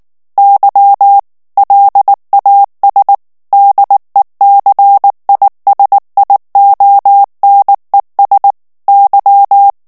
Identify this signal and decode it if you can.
Morse code, heard as a keyed tone, about 19 wpm, 800 Hz. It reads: Y LAS DECISIONES Y
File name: morse mp3 / ESPAGNOL TEXTE EN CLAIR 2 VITESSE 18 MOTS MINUTES